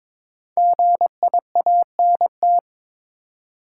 Morse code, heard as a keyed tone, about 23 wpm, 700 Hz. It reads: GIANT
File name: 5 Letter Words - First 3 Letters Given - 22wpm